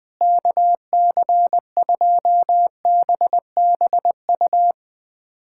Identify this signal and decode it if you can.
Morse code, heard as a keyed tone, about 21 wpm, 700 Hz. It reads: KC2BBU
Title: K1USN SST QSO Practice - 20wpm